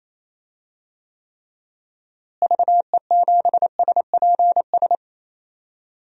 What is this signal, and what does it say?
Morse code, heard as a keyed tone, about 28 wpm, 700 Hz. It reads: VE7SPS
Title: DXCC-100 Call Signs - 6 Characters - Repeated 2x - 28wpm